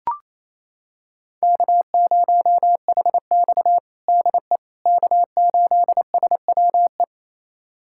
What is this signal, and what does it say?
Morse code, heard as a keyed tone, about 28 wpm, 700 Hz. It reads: K0HX DE K8SWE
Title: USA Callsign DE Callsign 28wpm